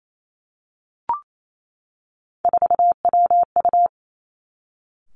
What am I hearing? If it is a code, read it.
Morse code, heard as a keyed tone, about 28 wpm, 700 Hz. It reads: 4WU
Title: Three Challenging Letter-Number Combinations 28wpm